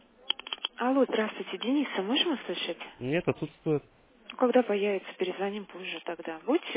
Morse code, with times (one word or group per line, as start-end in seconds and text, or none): none